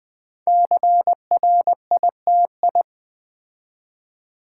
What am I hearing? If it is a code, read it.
Morse code, heard as a keyed tone, about 20 wpm, 700 Hz. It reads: CRITI
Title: Build A Word - 5 Letters At A Time - 10-Letter Words - 20wpm